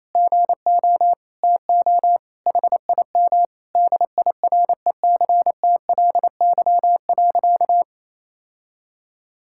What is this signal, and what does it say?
Morse code, heard as a keyed tone, about 28 wpm, 700 Hz. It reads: GO TO HIM DIRECTLY.
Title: Sentences from Top 1000 Words - Review Part 5 - 28wpm